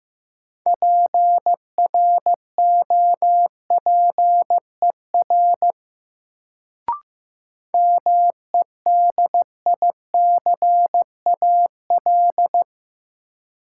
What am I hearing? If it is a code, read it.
Morse code, heard as a keyed tone, about 16 wpm, 700 Hz. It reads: PROPER MEDICAL
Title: Sentences from Top 800 Words - New Word Drill - 15wpm